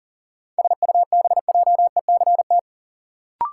Morse code, heard as a keyed tone, about 39 wpm, 700 Hz. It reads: SUBJECT
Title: Sentences from Top 800 Words - New Word Drill - 40wpm